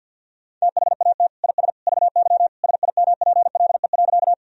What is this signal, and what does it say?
Morse code, heard as a keyed tone, about 53 wpm, 700 Hz. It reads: THAT IS VY SIMPLE.
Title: Sentences from Top 400 Words and Abbreviations 50wpm